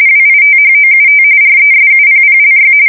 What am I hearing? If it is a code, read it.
RTTY, about 45.45 baud. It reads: WITHOUT RESUPPLY
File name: fsk170 / battleshipnj